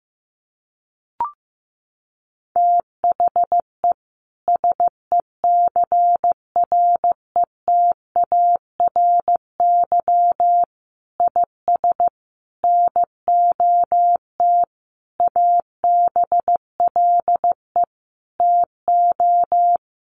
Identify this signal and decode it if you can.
Morse code, heard as a keyed tone, about 15 wpm, 700 Hz. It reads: THE SECRETARY IS NOT ABLE TO
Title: Sentences from Top 1300 Words 15wpm